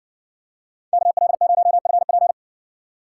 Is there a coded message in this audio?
Yes: Morse code from 0.9 to 2.3 s, heard as a keyed tone, about 56 wpm, 700 Hz. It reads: KL0FP